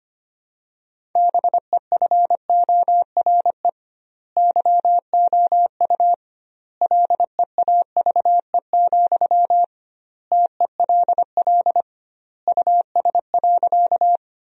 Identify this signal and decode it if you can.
Morse code, heard as a keyed tone, about 25 wpm, 700 Hz. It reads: BEFORE YOU LEAVE, TELL US.
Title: 2x Word-Spacing - Sentences from Top 400 Words - 25wpm